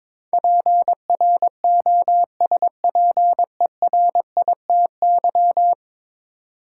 Morse code, heard as a keyed tone, about 22 wpm, 700 Hz. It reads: PROSPERITY
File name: practice / 22wpm / 10 Letter Words - First 3 Letters Given - 22wpm